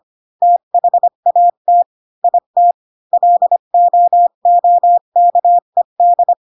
Morse code, heard as a keyed tone, about 25 wpm, 700 Hz. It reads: THAT IT LOOKED
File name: Pinocchio / Pinocchio ch1 20 25 ik1hge